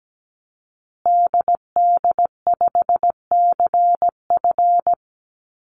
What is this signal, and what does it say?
Morse code, heard as a keyed tone, about 17 wpm, 700 Hz. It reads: DD5CF
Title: Call Signs Repeated 3x - 17wpm